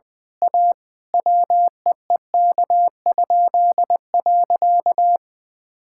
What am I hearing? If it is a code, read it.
Morse code, heard as a keyed tone, about 21 wpm, 700 Hz. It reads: A WEEK?.
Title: Pangram Challenge v2 - 20wpm